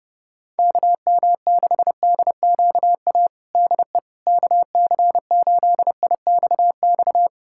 Morse code, heard as a keyed tone, about 30 wpm, 700 Hz. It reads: KM6DQA DE KC8IXX
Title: USA Callsign DE Callsign 30wpm